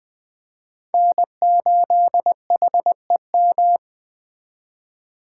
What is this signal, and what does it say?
Morse code, heard as a keyed tone, about 20 wpm, 700 Hz. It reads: N8HEM